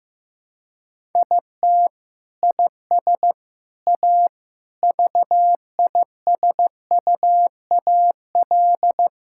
Morse code, heard as a keyed tone, about 15 wpm, 700 Hz. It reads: IT IS A VISUAL
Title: Sentences from Top 1000 Words - Review Part 6 - 15wpm